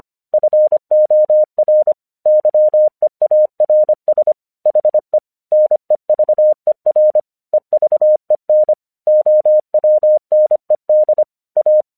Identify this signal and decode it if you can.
Morse code, heard as a keyed tone, about 25 wpm, 610 Hz. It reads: FOR YEARS HE NEVER EVEN OWNED A
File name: 1 English 25wpm / AQC08